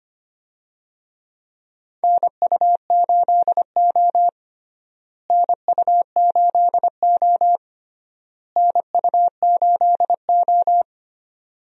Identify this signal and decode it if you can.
Morse code, heard as a keyed tone, about 25 wpm, 700 Hz. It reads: NU8O NU8O NU8O
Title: USA Call Signs - 4 Characters - Repeated 3x - 25wpm